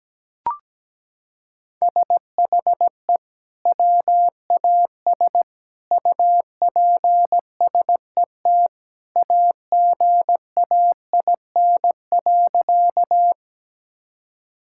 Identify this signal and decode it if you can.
Morse code, heard as a keyed tone, about 17 wpm, 700 Hz. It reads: SHE WAS UPSET AGAIN.
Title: Sentences from Top 1000 Words - Review Part 5 - 17wpm